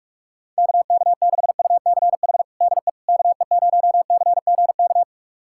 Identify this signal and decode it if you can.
Morse code, heard as a keyed tone, about 46 wpm, 700 Hz. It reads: KK6UCH DE KE0CGK
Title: USA Callsign DE Callsign 45wpm